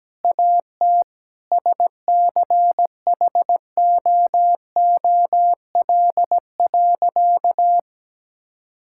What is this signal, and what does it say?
Morse code, heard as a keyed tone, about 17 wpm, 700 Hz. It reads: AT SCHOOL.
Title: Sentences from Top 1000 Words - Review Part 6 - 17wpm